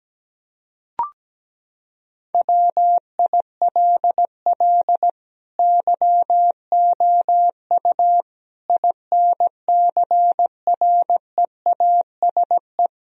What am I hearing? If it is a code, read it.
Morse code, heard as a keyed tone, about 17 wpm, 700 Hz. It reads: WILL YOU INCREASE
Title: Sentences from Top 1000 Words - Review Part 4 - 17wpm